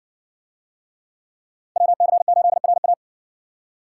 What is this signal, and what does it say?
Morse code, heard as a keyed tone, about 60 wpm, 700 Hz. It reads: WC8RA